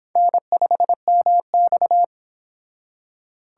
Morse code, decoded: N5MX

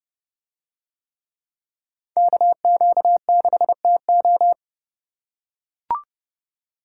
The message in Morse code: KQ6TO

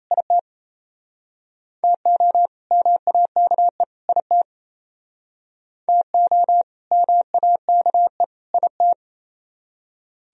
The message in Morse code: IT TO MAKE IT TO MAKE IT

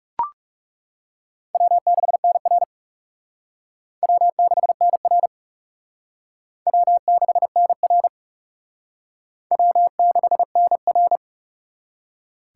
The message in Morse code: W6NR W6NR W6NR W6NR

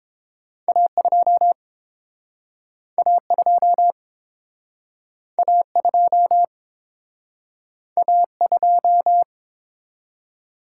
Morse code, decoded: A2 A2 A2 A2

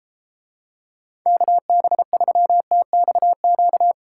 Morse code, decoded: KB3TXQ

